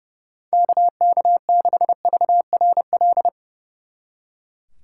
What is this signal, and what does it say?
Morse code, heard as a keyed tone, about 37 wpm, 700 Hz. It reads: KK6VRL
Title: Speed Racing - USA 2x3 Call Signs 30wpm